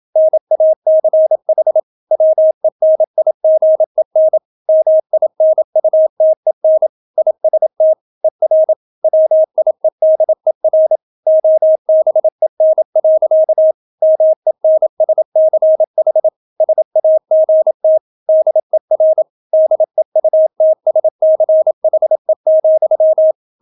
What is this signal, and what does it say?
Morse code, heard as a keyed tone, about 27 wpm, 630 Hz. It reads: NACH WENIGEN MINUTEN IST ER WIEDER OBEN. MENSCH SAGT DER DEUTSCHE,